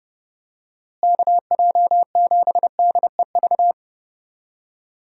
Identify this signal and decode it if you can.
Morse code, heard as a keyed tone, about 30 wpm, 700 Hz. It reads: KJ7DEV
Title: USA Call Signs - 6 Characters - Repeated 2x - 30wpm